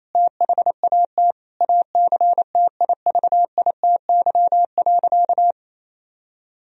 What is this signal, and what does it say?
Morse code, heard as a keyed tone, about 28 wpm, 700 Hz. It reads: THAT ACTIVITY.
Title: Sentences from Top 1000 Words - Review Part 4 - 28wpm